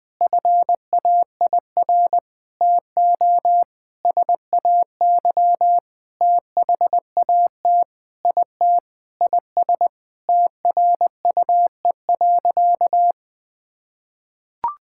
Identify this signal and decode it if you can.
Morse code, heard as a keyed tone, about 20 wpm, 700 Hz. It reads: FAIR TO SAY THAT IT IS TRUE.
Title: Sentences with 8-Word Phrases - Spoken Beginning - 20wpm